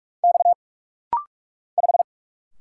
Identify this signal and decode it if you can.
Morse code, heard as a keyed tone, about 46 wpm, 700 Hz. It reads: X 5
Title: Single Letter-Number - Rapid-Fire 45wpm